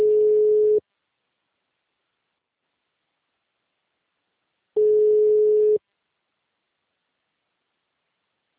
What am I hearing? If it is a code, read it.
Ringback tone (Europe, 425 Hz).